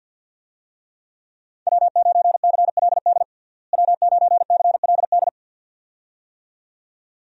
Morse code, heard as a keyed tone, about 52 wpm, 700 Hz. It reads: W9CLD W9CLD
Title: USA Call Signs - 5 Characters - Repeated 2x - 50wpm